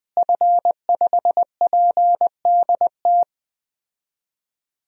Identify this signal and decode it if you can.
Morse code, heard as a keyed tone, about 20 wpm, 700 Hz. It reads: F5PDT
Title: DXCC-100 Call Signs - 5 Characters - Repeated 2x - 20wpm